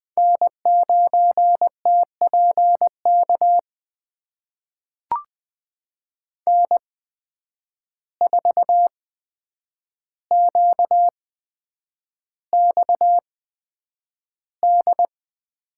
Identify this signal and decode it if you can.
Morse code, heard as a keyed tone, about 20 wpm, spaced as for 9 wpm, 700 Hz. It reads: N9TPK N4QXD
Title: USA Call Signs - 5 Characters - 8x Character Spacing - 20wpm